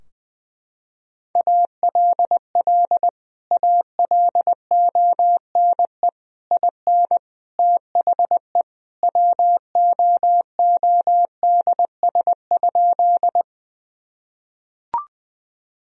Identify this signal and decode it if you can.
Morse code, heard as a keyed tone, about 20 wpm, 700 Hz. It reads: ALL ALONE IN THE WOODS?
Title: Sentences from Top 1000 Words - Review Part 2 - 20wpm